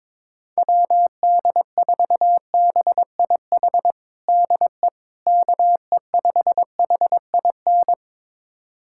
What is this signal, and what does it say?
Morse code, heard as a keyed tone, about 22 wpm, 700 Hz. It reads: WD4BIH DE KE5HIN
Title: USA Callsign DE Callsign 22wpm